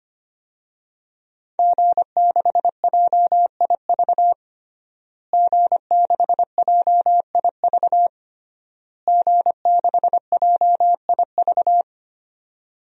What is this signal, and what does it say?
Morse code, heard as a keyed tone, about 25 wpm, 700 Hz. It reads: G6JIV G6JIV G6JIV
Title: DXCC-100 Call Signs - 5 Characters - Repeated 3x - 25wpm